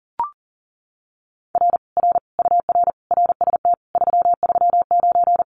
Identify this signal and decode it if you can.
Morse code, heard as a keyed tone, about 37 wpm, 700 Hz. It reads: R R UR RST 339
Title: Common QSO Phrases Encore 40wpm